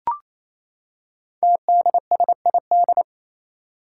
Morse code, heard as a keyed tone, about 27 wpm, 700 Hz. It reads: TDSID
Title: Five Challenging Letter-Number Combinations 28wpm